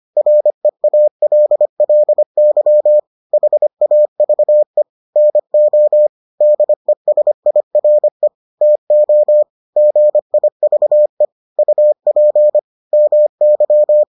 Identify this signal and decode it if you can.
Morse code, heard as a keyed tone, about 25 wpm, 600 Hz. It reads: REALLY HAVE NO DESIRE TO GIVE UP MY